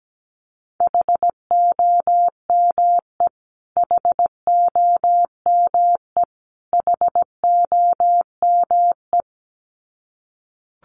Morse code, heard as a keyed tone, about 17 wpm, 700 Hz. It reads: HOME HOME HOME
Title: Top 300 Words Repeated 17wpm